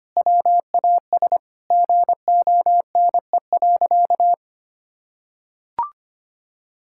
Morse code, heard as a keyed tone, about 25 wpm, 700 Hz. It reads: WAS GONE.
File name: practice / 25wpm / Sentences with 3-Word Phrases - 25wpm